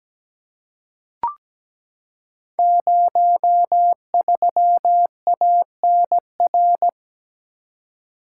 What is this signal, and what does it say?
Morse code, heard as a keyed tone, about 17 wpm, 700 Hz. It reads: 03ANR
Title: Code Groups - 5 Letter:Numbers - 17wpm